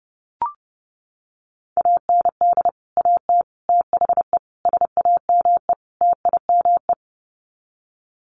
Morse code, heard as a keyed tone, about 30 wpm, 700 Hz. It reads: AND AT THE SAME TIME
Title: Sets of 5 Words 30wpm